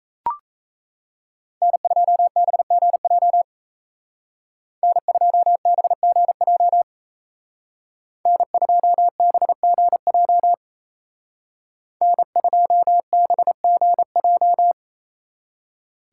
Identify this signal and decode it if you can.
Morse code, heard as a keyed tone, about 36 wpm, 700 Hz. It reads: N2BGJ N2BGJ N2BGJ N2BGJ